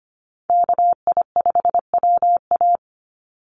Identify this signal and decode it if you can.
Morse code, heard as a keyed tone, about 25 wpm, 700 Hz. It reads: KI5WA
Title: NA SOTA Chaser Callsigns 25wpm